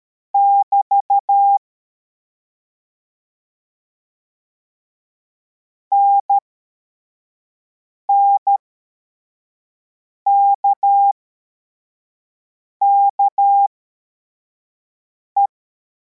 Morse code, heard as a keyed tone, about 13 wpm, spaced as for 4 wpm, 800 Hz. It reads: <BT> NNKKE